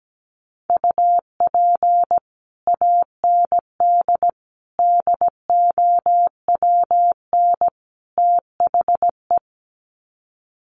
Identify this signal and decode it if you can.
Morse code, heard as a keyed tone, about 17 wpm, 700 Hz. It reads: UP AND DOWN THE
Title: Sets of 4 Words 17wpm